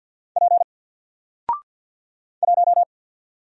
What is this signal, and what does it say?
Morse code, heard as a keyed tone, about 49 wpm, 700 Hz. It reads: P 1
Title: Single Character (Letter:Number:Punctuation:Prosign) Alt 50wpm